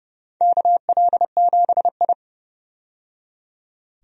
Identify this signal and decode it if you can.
Morse code, heard as a keyed tone, about 32 wpm, 700 Hz. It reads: KL7I
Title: DXCC-100 Call Signs - 4 Characters - 30wpm